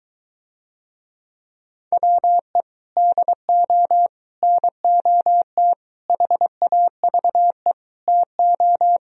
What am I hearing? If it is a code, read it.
Morse code, heard as a keyed tone, about 22 wpm, 700 Hz. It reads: WE DO NOT HAVE TO